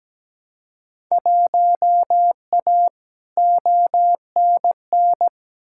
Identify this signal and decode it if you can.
Morse code, heard as a keyed tone, about 17 wpm, 700 Hz. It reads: 1A ONN
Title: Field Day Exchanges Encore - 17wpm